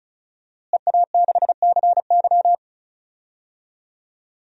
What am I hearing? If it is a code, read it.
Morse code, heard as a keyed tone, about 34 wpm, 700 Hz. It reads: EA6CY
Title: DXCC-100 Call Signs - 5 Characters - Repeated 2x - 35wpm